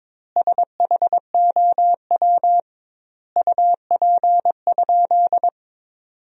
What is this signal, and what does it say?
Morse code, heard as a keyed tone, about 22 wpm, 700 Hz. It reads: SHOW UP?
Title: 2x Word-Spacing - Sentences from Top 200 Words - 22wpm